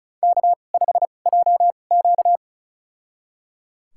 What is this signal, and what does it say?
Morse code, heard as a keyed tone, about 36 wpm, spaced as for 25 wpm, 690 Hz. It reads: K5JQ
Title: USA Call Signs - 4 Characters - 2x Character Spacing - 35wpm